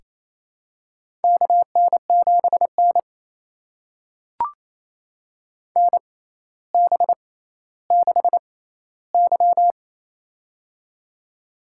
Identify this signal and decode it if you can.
Morse code, heard as a keyed tone, about 28 wpm, spaced as for 14 wpm, 700 Hz. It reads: KN7N NB6Y
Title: USA Call Signs - 6x Character Spacing - 28wpm